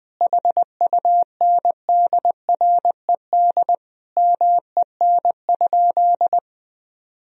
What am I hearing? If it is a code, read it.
Morse code, heard as a keyed tone, about 20 wpm, 700 Hz. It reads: HUNDRED MEN?
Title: Sentences from Top 1000 Words - Review Part 2 - 20wpm